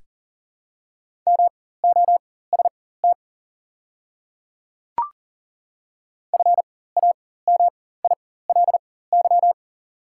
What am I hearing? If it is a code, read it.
Morse code, heard as a keyed tone, about 38 wpm, spaced as for 18 wpm, 690 Hz. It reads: MOST FAMILY
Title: 4x Character Spacing - Top 500 Words - 40wpm